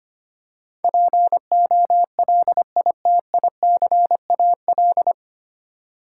Morse code, heard as a keyed tone, about 25 wpm, 700 Hz. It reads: POLITICAL
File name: Nine Letter Words - 25wpm